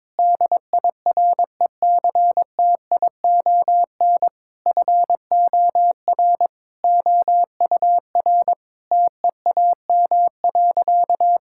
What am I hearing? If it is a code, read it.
Morse code, heard as a keyed tone, about 22 wpm, 700 Hz. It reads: DIRECTION FOR OUR TEAM.
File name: practice / 22wpm / Sentences with 8-Word Phrases - 22wpm